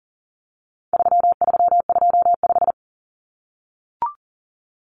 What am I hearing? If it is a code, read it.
Morse code, heard as a keyed tone, about 40 wpm, 700 Hz. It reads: 3325
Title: Common Cut Numbers - 4-Digit 40wpm